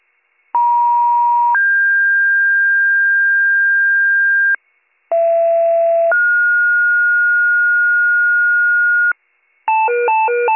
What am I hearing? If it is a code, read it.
Two-tone page, heard as a one-second tone, then a three-second tone, about 950 then 1600 Hz, about 670 then 1430 Hz, with noise. It is heard from 0.5 to 9.1 s.